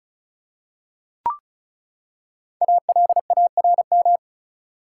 Morse code, heard as a keyed tone, about 36 wpm, 700 Hz. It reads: ALARM
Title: Five Letter Words - 35wpm